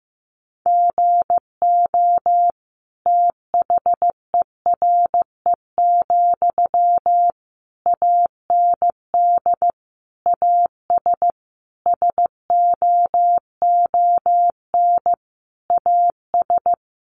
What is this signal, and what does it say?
Morse code, heard as a keyed tone, about 15 wpm, 700 Hz. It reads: GO THERE, AND AS SOON AS
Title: Sentences from Top 1300 Words 15wpm